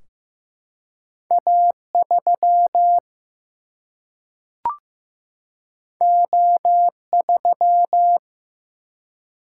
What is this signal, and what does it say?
Morse code, heard as a keyed tone, about 15 wpm, 700 Hz. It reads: A3 O3
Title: Single-Letter Single-Number 15wpm